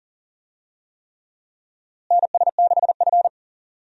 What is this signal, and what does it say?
Morse code, heard as a keyed tone, about 40 wpm, 700 Hz. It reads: NS6F
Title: USA Call Signs - 4 Characters - Repeated 3x - 40wpm